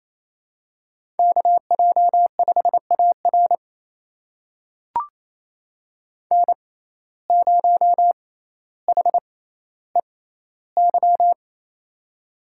Morse code, heard as a keyed tone, about 27 wpm, spaced as for 14 wpm, 700 Hz. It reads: KJ5AR N0HEY
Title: USA Call Signs - 5 Characters - 6x Character Spacing - 28wpm